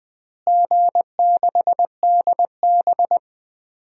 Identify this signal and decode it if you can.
Morse code, heard as a keyed tone, about 20 wpm, 700 Hz. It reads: G6DB
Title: DXCC-100 Call Signs - 4 Characters - Repeated 2x - 20wpm